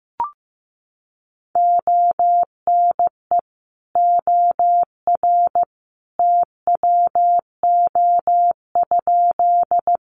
Morse code, heard as a keyed tone, about 15 wpm, 700 Hz. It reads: ONE OR TWO?
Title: Sentences from Top 100 Words 15wpm